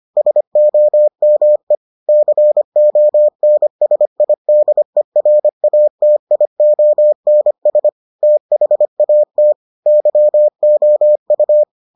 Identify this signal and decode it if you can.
Morse code, heard as a keyed tone, about 25 wpm, 600 Hz. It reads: SOME CONSIDERATIONS THAT YOU